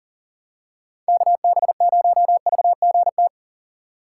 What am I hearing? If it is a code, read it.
Morse code, heard as a keyed tone, about 39 wpm, 700 Hz. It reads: KB0VGT
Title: USA Call Signs - 6 Characters - 3x Character Spacing - 40wpm